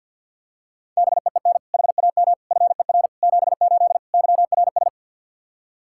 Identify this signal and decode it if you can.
Morse code, heard as a keyed tone, about 48 wpm, 700 Hz. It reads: BEEN HAM FER 78 YRS
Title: Practice QSO - Exchange 3 - More Info - 50wpm